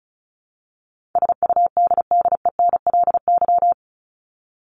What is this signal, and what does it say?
Morse code, heard as a keyed tone, about 34 wpm, 700 Hz. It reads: SUDDENLY
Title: LY Suffix - Top 100 Words with Base Words 35wpm